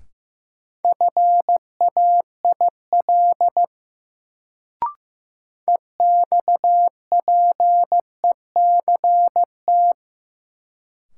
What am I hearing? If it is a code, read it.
Morse code, heard as a keyed tone, about 15 wpm, 700 Hz. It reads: FAIL EXPECT
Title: Sentences from Top 1100 Words - New Word Drill - 15wpm